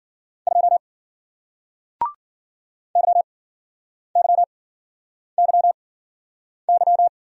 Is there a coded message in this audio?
Yes: Morse code from 0.5 to 7.1 s, heard as a keyed tone, about 54 wpm, spaced as for 9 wpm, 700 Hz. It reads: 2 YYYY